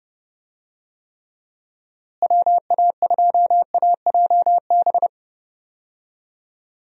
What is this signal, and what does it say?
Morse code, heard as a keyed tone, about 30 wpm, 700 Hz. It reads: WA2AJB
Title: USA Call Signs - 6 Characters - Repeated 3x - 30wpm